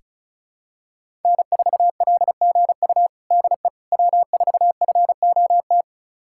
Morse code, heard as a keyed tone, about 34 wpm, 700 Hz. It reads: N4LGU DE W4FOT
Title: USA Callsign DE Callsign - 5 Characters - 35wpm